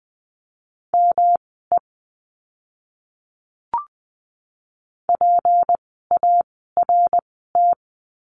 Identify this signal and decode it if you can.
Morse code, heard as a keyed tone, about 20 wpm, spaced as for 14 wpm, 700 Hz. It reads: ME PART
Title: 230 Lesson 37f - Letter x - Words 20wpm